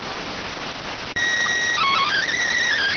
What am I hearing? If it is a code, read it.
SSTV (slow-scan television), PD 290.